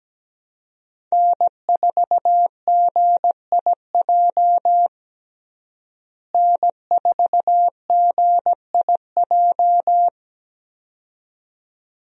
Morse code, heard as a keyed tone, about 17 wpm, 700 Hz. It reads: N4GIJ N4GIJ
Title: USA Call Signs - 5 Characters - Repeated 2x - 17wpm